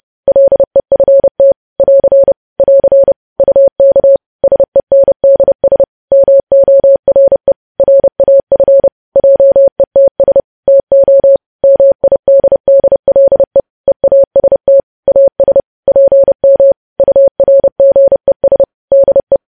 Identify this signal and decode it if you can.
Morse code, heard as a keyed tone, about 30 wpm, 550 Hz. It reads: LEFT <AR> <AR> UK SENDS MORE RAF JETS TO MIDDLE EAST AS PM URGES DE
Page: News Headlines in Morse Code at 30 WPM